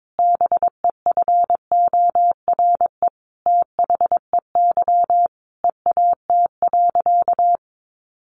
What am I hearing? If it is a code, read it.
Morse code, heard as a keyed tone, about 22 wpm, 700 Hz. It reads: BEFORE THEY EAT.
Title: Sentences from Top 700 Words 22wpm